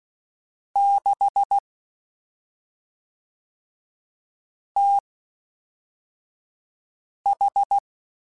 Morse code, heard as a keyed tone, about 16 wpm, spaced as for 3 wpm, 800 Hz. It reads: 6TH
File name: CW Comparisons / CW Comparisons-mp3s / CW Comparisons Broken Characters @5